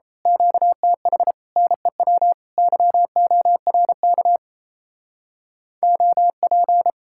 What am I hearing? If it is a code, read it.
Morse code, heard as a keyed tone, about 29 wpm, 700 Hz. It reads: QTH NEW YORK OP
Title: Speed-Racing Common QSO Phrases Encore - 28wpm